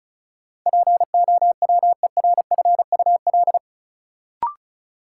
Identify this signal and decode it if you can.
Morse code, heard as a keyed tone, about 34 wpm, 700 Hz. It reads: POWERFUL
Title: Sets of 5 Words Encore - New Word Drill - 35wpm